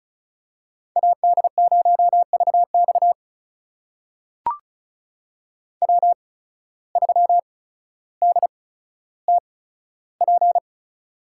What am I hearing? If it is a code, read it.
Morse code, heard as a keyed tone, about 36 wpm, spaced as for 15 wpm, 700 Hz. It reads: AD0VX W3DTP
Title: USA Call Signs - 5 Characters - 8x Character Spacing - 35wpm